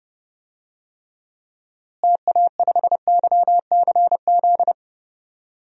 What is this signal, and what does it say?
Morse code, heard as a keyed tone, about 30 wpm, 700 Hz. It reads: TA5YCZ
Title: DXCC-100 Call Signs - 6 Characters - Repeated 3x - 30wpm